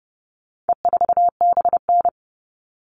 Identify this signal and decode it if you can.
Morse code, heard as a keyed tone, about 30 wpm, 700 Hz. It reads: E4BN
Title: All Cut Numbers - 4-Digit 30wpm